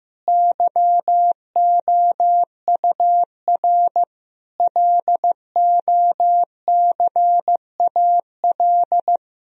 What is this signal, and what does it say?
Morse code, heard as a keyed tone, about 15 wpm, 700 Hz. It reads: YOUR LOCAL
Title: Sentences from Top 1000 Words - Review Part 3 - 15wpm